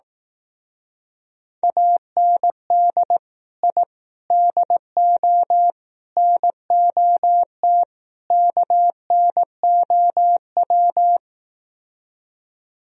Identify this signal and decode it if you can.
Morse code, heard as a keyed tone, about 18 wpm, 700 Hz. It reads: AND I DO NOT KNOW
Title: Speed-Racing - Sets of 5 Words 15wpm